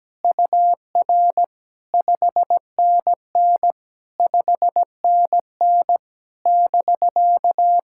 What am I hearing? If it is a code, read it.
Morse code, heard as a keyed tone, about 17 wpm, 700 Hz. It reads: UR 5NN 5NN <BK>
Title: SOTA QSO Practice 17wpm